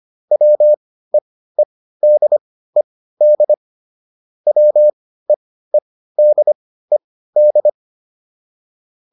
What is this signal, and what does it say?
Morse code, heard as a keyed tone, about 25 wpm, spaced as for 15 wpm, 600 Hz. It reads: WEEDED WEEDED